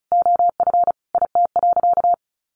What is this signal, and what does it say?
Morse code, heard as a keyed tone, about 34 wpm, 700 Hz. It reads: OF IT.
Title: Sentences from Top 1400 Words 35wpm